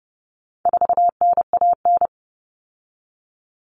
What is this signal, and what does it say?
Morse code, heard as a keyed tone, about 30 wpm, 700 Hz. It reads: 4NAN
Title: Common Cut Numbers - 4-Digit 30wpm